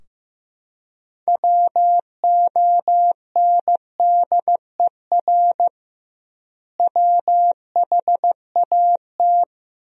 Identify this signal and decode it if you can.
Morse code, heard as a keyed tone, about 15 wpm, 700 Hz. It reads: WONDER WHAT
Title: 2x Word-Spacing - Sentences from Top 500 Words - 15wpm